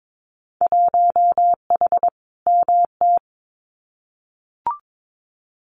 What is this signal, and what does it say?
Morse code, heard as a keyed tone, about 22 wpm, 700 Hz. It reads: 1H MT
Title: Winter Field Day Exchanges Encore 22wpm